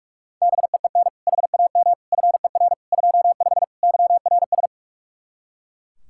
Morse code, heard as a keyed tone, about 41 wpm, 700 Hz. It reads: BEEN HAM FER 25 YRS
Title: K1USN SST QSO Practice 45wpm